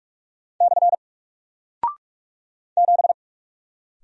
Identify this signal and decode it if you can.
Morse code, heard as a keyed tone, about 44 wpm, 700 Hz. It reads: / 7
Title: Single Character (Letter:Number:Punctuation:Prosign) Alt 45wpm